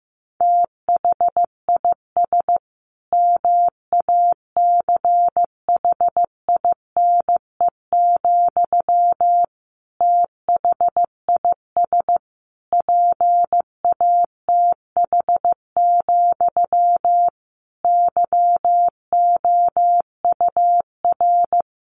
Morse code, heard as a keyed tone, about 15 wpm, 700 Hz. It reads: THIS MACHINE, THIS PATH, YOUR
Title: A Sound Of Thunder 15wpm